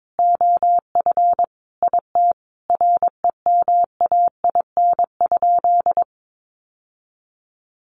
Morse code, heard as a keyed tone, about 22 wpm, 700 Hz. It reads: OF IT REMAIN?
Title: Sentences from Top 1200 Words 22wpm